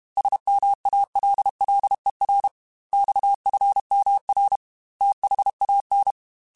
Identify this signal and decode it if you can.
Morse code, heard as a keyed tone, about 32 wpm, 800 Hz. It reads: SMALLER XFMR THAN
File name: Technical Subjects / Technical Subjects-mp3s / Sw Pwr Sup @30